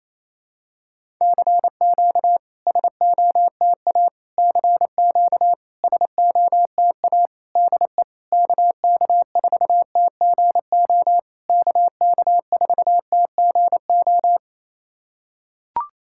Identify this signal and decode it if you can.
Morse code, heard as a keyed tone, about 28 wpm, 700 Hz. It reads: CQ SOTA CQ SOTA DE KK4TGO KK4TGO
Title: SOTA QSO Practice 28wpm